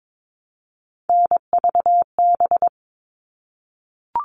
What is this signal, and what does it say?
Morse code, heard as a keyed tone, about 22 wpm, 700 Hz. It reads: NVB
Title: All Cut Numbers - 3-Digit 22wpm